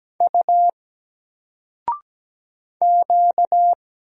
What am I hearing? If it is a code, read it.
Morse code, heard as a keyed tone, about 17 wpm, 700 Hz. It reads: U Q
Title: Single Letter-Number Alt 17wpm